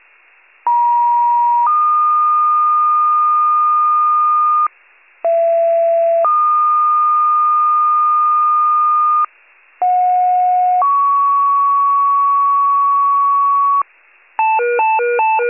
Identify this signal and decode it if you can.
Two-tone page, heard as a one-second tone, then a three-second tone, about 950 then 1220 Hz, about 670 then 1220 Hz, about 730 then 1120 Hz. It is heard from 0.7 to 13.8 s.
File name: pager / 83 RESCUE OFC / wcalert